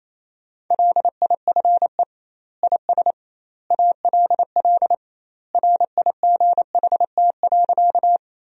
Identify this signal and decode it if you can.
Morse code, heard as a keyed tone, about 28 wpm, 700 Hz. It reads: LIFE IS ALL RIGHT.
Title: 2x Word-Spacing - Sentences from Top 300 Words - 28wpm